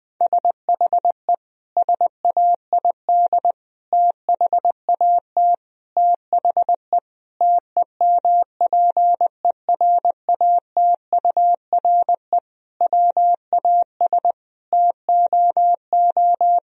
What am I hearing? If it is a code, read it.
Morse code, heard as a keyed tone, about 20 wpm, 700 Hz. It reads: SHE SAID THAT THE TEMPERATURE WAS TOO
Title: Sentences from Top 1000 Words - Review Part 4 - 20wpm